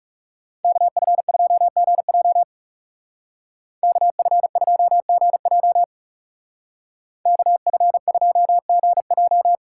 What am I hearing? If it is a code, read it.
Morse code, heard as a keyed tone, about 34 wpm, 700 Hz. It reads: KF2GJ KF2GJ KF2GJ
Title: Speed Racing - USA Call Signs 30wpm